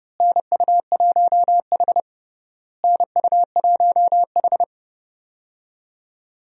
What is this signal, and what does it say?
Morse code, heard as a keyed tone, about 30 wpm, 700 Hz. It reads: NU1H NU1H
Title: USA Call Signs - 4 Characters - Repeated 3x - 30wpm